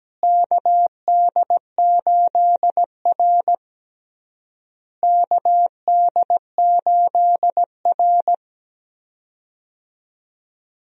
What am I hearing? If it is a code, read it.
Morse code, heard as a keyed tone, about 17 wpm, 700 Hz. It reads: KD8R KD8R
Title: USA Call Signs - 4 Characters - Repeated 3x - 17wpm